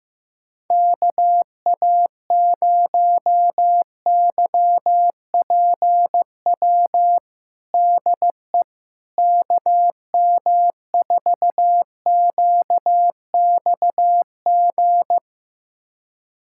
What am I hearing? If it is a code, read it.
Morse code, heard as a keyed tone, about 15 wpm, 700 Hz. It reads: KA0YPW DE KM4QXG
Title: USA Callsign DE Callsign - 6 Characters - 15wpm